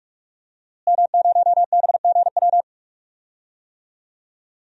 Morse code, heard as a keyed tone, about 46 wpm, 700 Hz. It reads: M0BGW
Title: DXCC-100 Call Signs - 5 Characters - Repeated 2x - 45wpm